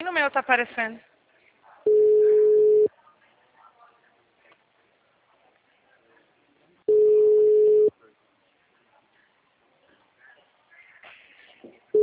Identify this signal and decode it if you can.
Ringback tone (Europe, 425 Hz), with speech.